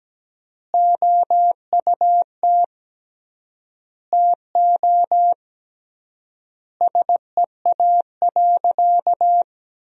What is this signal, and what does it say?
Morse code, heard as a keyed tone, about 17 wpm, 700 Hz. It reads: OUT TO SEA.
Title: 3x Word-Spacing - Sentences from Top 400 Words - 17wpm